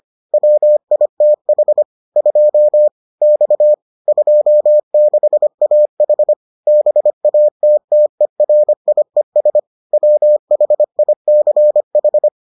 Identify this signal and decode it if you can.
Morse code, heard as a keyed tone, about 25 wpm, 600 Hz. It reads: WITH 2 X 26AH BATTERIES WHICH